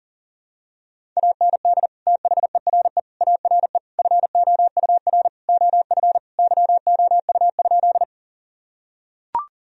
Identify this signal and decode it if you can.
Morse code, heard as a keyed tone, about 37 wpm, 700 Hz. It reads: AND THERE ARE FOUR OF YOU?
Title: Sentences from Top 1000 Words - Review Part 1 - 40wpm